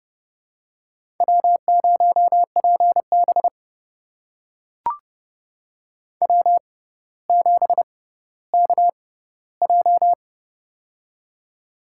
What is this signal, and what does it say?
Morse code, heard as a keyed tone, about 30 wpm, spaced as for 15 wpm, 700 Hz. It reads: W0PB W7KJ
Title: USA Call Signs - 6x Character Spacing - 30wpm